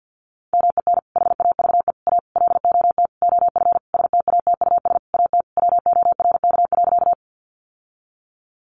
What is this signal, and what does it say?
Morse code, heard as a keyed tone, about 50 wpm, 700 Hz. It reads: MEN HAVE A LOT OF STATUS AT WORK.
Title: Sentences from Top 1400 Words 50wpm